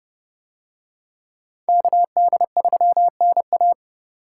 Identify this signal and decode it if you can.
Morse code, heard as a keyed tone, about 27 wpm, 700 Hz. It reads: KD3NA